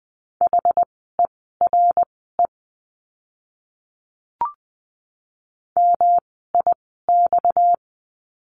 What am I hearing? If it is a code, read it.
Morse code, heard as a keyed tone, about 20 wpm, spaced as for 14 wpm, 700 Hz. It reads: HERE MIX
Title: 230 Lesson 37f - Letter x - Words 20wpm